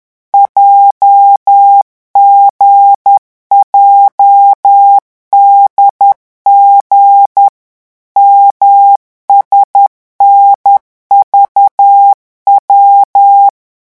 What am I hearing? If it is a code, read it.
Morse code, heard as a keyed tone, about 10 wpm, 800 Hz. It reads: JGJDG MSNVW